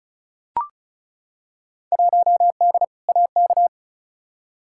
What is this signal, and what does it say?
Morse code, heard as a keyed tone, about 36 wpm, 700 Hz. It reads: 1D AK
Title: Field Day Exchanges - 35wpm